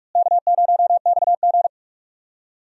Morse code, heard as a keyed tone, about 42 wpm, 700 Hz. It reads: K0XG
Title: USA Call Signs - 4 Characters - Repeated 3x - 45wpm